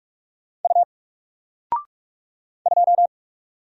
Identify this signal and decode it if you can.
Morse code, heard as a keyed tone, about 44 wpm, 700 Hz. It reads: U 2
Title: Single Character (Letter:Number:Punctuation:Prosign) 45wpm